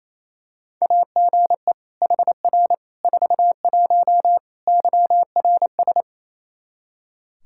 Morse code, heard as a keyed tone, about 28 wpm, 700 Hz. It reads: AGE HR 41 YRS
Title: Practice QSO - Exchange 3 - More Info - 28wpm